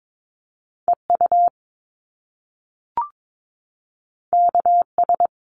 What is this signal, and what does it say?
Morse code, heard as a keyed tone, about 22 wpm, 700 Hz. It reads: EU KS
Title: Top Two Letter Combinations in Words 22wpm